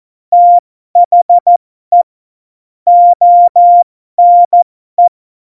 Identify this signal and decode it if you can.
Morse code, heard as a keyed tone, about 14 wpm, spaced as for 12 wpm, 700 Hz. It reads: THE ONE
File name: Morsekurs / Witze